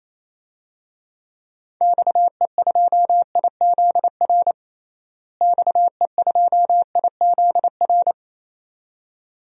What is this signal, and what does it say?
Morse code, heard as a keyed tone, about 28 wpm, 700 Hz. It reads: XE2IZR XE2IZR
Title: DXCC-100 Call Signs - 6 Characters - Repeated 2x - 28wpm